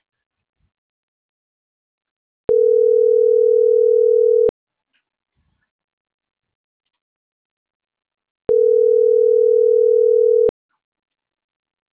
Ringback tone (North America), 440 + 480 Hz.